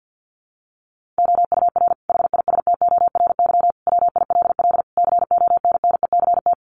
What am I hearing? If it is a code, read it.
Morse code, heard as a keyed tone, about 50 wpm, 700 Hz. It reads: OUR HISTORY WILL CONNECT
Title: Sentences from Top 1500 Words 50wpm